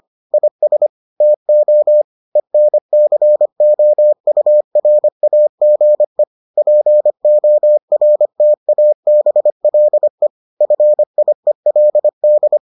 Morse code, heard as a keyed tone, about 25 wpm, 600 Hz. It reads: IS TO ENCOURAGE PORTABLE FIELD